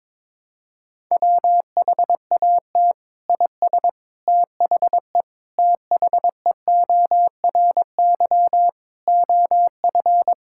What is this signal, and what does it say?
Morse code, heard as a keyed tone, about 22 wpm, 700 Hz. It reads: WHAT IS THE THEORY OF
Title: Sentences from Top 1000 Words - Review Part 6 - 22wpm